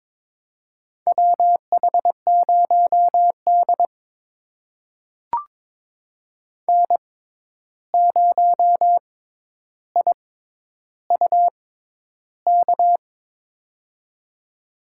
Morse code, heard as a keyed tone, about 22 wpm, spaced as for 11 wpm, 700 Hz. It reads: WH0D N0IUK
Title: USA Call Signs - 6x Character Spacing - 22wpm